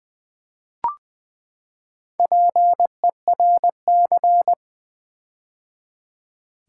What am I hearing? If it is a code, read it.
Morse code, heard as a keyed tone, about 20 wpm, 700 Hz. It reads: PERC
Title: Build A Word - 4 Letters At A Time - 8-Letter Words - 20wpm